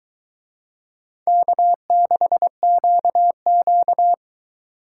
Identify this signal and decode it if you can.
Morse code, heard as a keyed tone, about 23 wpm, 700 Hz. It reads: K6QQ